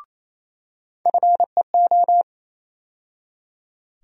Morse code, heard as a keyed tone, about 28 wpm, 700 Hz. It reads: FEO
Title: USA Call Sign Postfixes - 3 Letters 28wpm